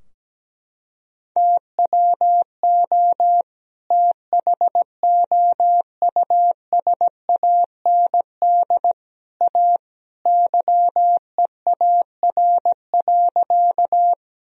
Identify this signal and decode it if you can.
Morse code, heard as a keyed tone, about 17 wpm, 700 Hz. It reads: TWO THOUSAND A YEAR.
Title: Sentences from Top 1000 Words - Review Part 3 - 17wpm